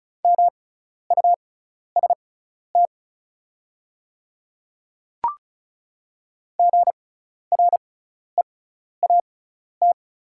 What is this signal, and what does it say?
Morse code, heard as a keyed tone, about 36 wpm, spaced as for 12 wpm, 700 Hz. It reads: MUST GREAT
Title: 6x Character Spacing - Top 200 Words - 35wpm